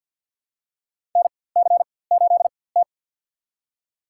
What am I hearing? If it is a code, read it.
Morse code, heard as a keyed tone, about 52 wpm, spaced as for 24 wpm, 700 Hz. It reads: NC8T
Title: USA Call Signs - 4x Character Spacing - 50wpm